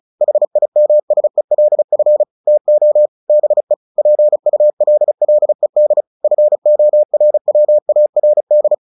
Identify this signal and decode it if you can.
Morse code, heard as a keyed tone, about 34 wpm, 600 Hz. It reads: HIMSELF TO BE PULLED FORWARD
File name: the old man and the sea 2 35wpm